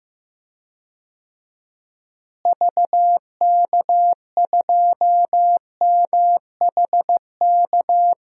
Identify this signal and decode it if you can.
Morse code, heard as a keyed tone, about 15 wpm, 700 Hz. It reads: VK2MHK
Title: DXCC-100 Call Signs - 6 Characters - Repeated 2x - 15wpm